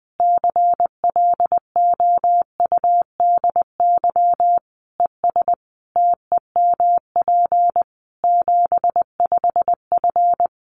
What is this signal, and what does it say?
Morse code, heard as a keyed tone, about 20 wpm, 700 Hz. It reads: CLOUDY ES TEMP 75F
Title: Common QSO Phrases 20wpm